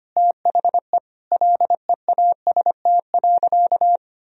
Morse code, heard as a keyed tone, about 25 wpm, 700 Hz. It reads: THE LEAST.
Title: Sentences from Top 1000 Words - Review Part 4 - 25wpm